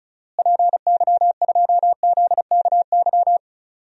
Morse code, heard as a keyed tone, about 36 wpm, 700 Hz. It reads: PY2ZKY
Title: DXCC-100 Call Signs - 6 Characters - Repeated 3x - 35wpm